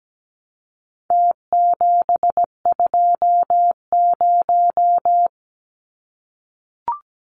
Morse code, heard as a keyed tone, about 17 wpm, 700 Hz. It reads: T720
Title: Numbers - Contest Style - 4-Digit 17wpm